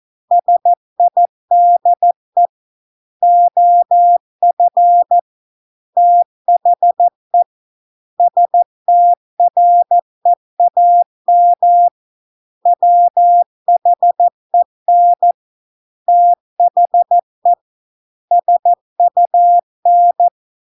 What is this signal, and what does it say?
Morse code, heard as a keyed tone, about 14 wpm, 700 Hz. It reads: SIDE OF THE STREAM WHEN THE SUN